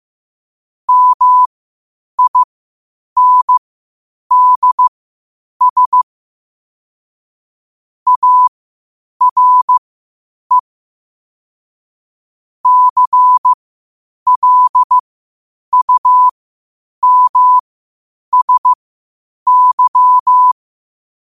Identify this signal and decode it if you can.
Morse code, heard as a keyed tone, about 15 wpm, spaced as for 8 wpm, 1000 Hz. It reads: MINDS ARE CLUMSY
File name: qotdfiles / 8wpm